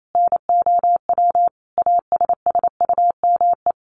Morse code, heard as a keyed tone, about 28 wpm, 700 Hz. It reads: NOW ASSUME
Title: Sentences from Top 1500 Words 28wpm